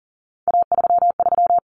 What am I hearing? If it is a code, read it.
Morse code, heard as a keyed tone, about 40 wpm, 700 Hz. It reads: A33
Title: Common Cut Numbers - 3-Digit 40wpm